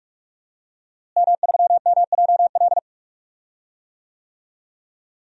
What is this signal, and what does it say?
Morse code, heard as a keyed tone, about 42 wpm, 700 Hz. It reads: M3MJL
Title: DXCC-100 Call Signs - 5 Characters - Repeated 2x - 45wpm